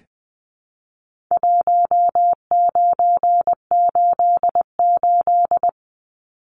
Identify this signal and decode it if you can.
Morse code, heard as a keyed tone, about 20 wpm, 700 Hz. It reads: 1988
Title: Numbers - Contest Style - 4-Digit 20wpm